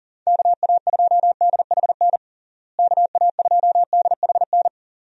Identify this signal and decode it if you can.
Morse code, heard as a keyed tone, about 40 wpm, 700 Hz. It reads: KA2DHN KA2DHN
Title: USA Call Signs - 6 Characters - Repeated 2x - 40wpm